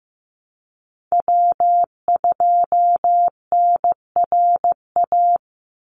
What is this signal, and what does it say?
Morse code, heard as a keyed tone, about 15 wpm, 700 Hz. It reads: W2NRA
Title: Call Signs Repeated 3x - 15wpm